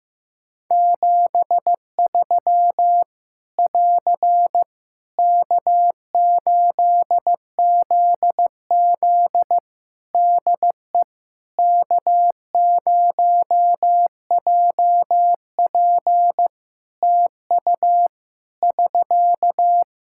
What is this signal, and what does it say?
Morse code, heard as a keyed tone, about 15 wpm, 700 Hz. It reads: 73 <AR> K8ZZ DE K0JP TU <SK>
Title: Practice QSO - Exchange 4 - Ending - 15wpm